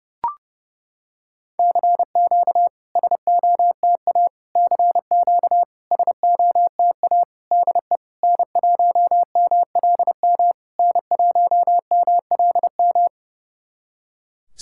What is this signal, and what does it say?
Morse code, heard as a keyed tone, about 30 wpm, 690 Hz. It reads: CQ SOTA CQ SOTA DE N1MLM N1MLM
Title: SOTA QSO Practice 30wpm